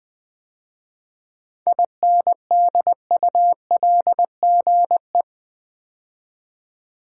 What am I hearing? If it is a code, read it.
Morse code, heard as a keyed tone, about 20 wpm, 690 Hz. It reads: INDULGE
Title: 7 Letter Words - First 3 Letters Given - 20wpm